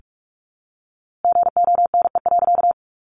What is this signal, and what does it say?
Morse code, heard as a keyed tone, about 45 wpm, 700 Hz. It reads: GONE.
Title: A Sound Of Thunder 45wpm